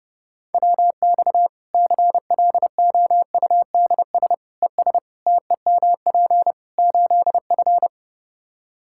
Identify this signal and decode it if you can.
Morse code, heard as a keyed tone, about 30 wpm, 700 Hz. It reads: WX CLOUDS ES TEMP 8F
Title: Practice QSO - Exchange 2 - Info - 30wpm